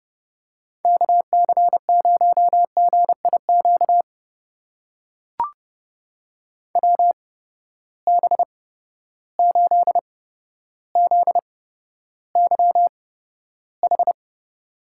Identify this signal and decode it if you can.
Morse code, heard as a keyed tone, about 31 wpm, spaced as for 13 wpm, 700 Hz. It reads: KC0GIQ WB8ZYH
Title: USA Call Signs - 8x Character Spacing - 30wpm